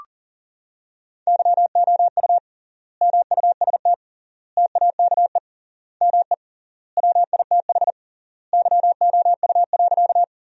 Morse code, heard as a keyed tone, about 40 wpm, 700 Hz. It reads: YOU MUST TAKE ME WITH YOU.
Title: 3x Word-Spacing - Sentences from Top 200 Words - 40wpm